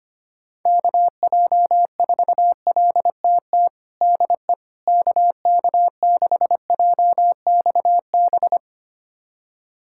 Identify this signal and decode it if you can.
Morse code, heard as a keyed tone, about 25 wpm, 700 Hz. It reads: KJ4LTT DE KK6JXB
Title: USA Callsign DE Callsign 25wpm